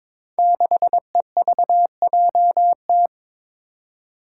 Morse code, heard as a keyed tone, about 22 wpm, 700 Hz. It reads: 6EVJT
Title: Five Challenging Letter-Number Combinations 22wpm